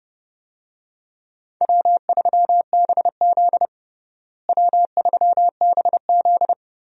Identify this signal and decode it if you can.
Morse code, heard as a keyed tone, about 30 wpm, 700 Hz. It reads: W3BZ W3BZ
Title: USA Call Signs - 4 Characters - Repeated 3x - 30wpm